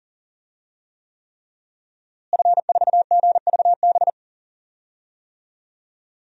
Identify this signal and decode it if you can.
Morse code, heard as a keyed tone, about 37 wpm, 700 Hz. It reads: F4GVB
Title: DXCC-100 Call Signs - 5 Characters - Repeated 3x - 40wpm